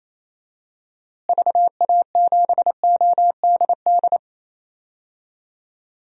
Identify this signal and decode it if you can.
Morse code, heard as a keyed tone, about 28 wpm, 700 Hz. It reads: VA7ODD
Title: DXCC-100 Call Signs - 6 Characters - Repeated 3x - 28wpm